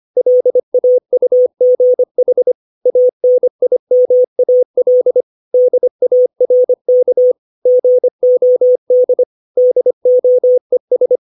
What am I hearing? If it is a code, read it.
Morse code, heard as a keyed tone, about 25 wpm, 500 Hz. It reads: LAUGH ANIMAL DARK GOD DOES
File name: 632 words / 632 Words 25WPM